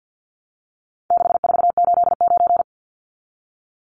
Morse code, heard as a keyed tone, about 50 wpm, 700 Hz. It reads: 6489
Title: Numbers - 4-Digit 50wpm